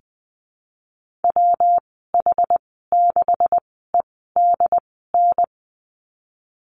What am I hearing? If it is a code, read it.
Morse code, heard as a keyed tone, about 20 wpm, spaced as for 14 wpm, 700 Hz. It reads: WH6EDN
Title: 161 Lesson 26g - Letter g - Call Signs 20wpm